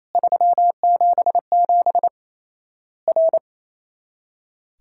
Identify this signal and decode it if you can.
Morse code, heard as a keyed tone, about 28 wpm, 700 Hz. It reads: 377 R
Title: Worked All Europe DX Contest - QTC Exchange - 28wpm